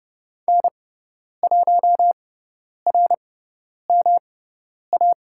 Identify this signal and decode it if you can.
Morse code, heard as a keyed tone, about 30 wpm, spaced as for 10 wpm, 700 Hz. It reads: N1RMA
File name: Lesson 18g - Letter m - Call Signs 30wpm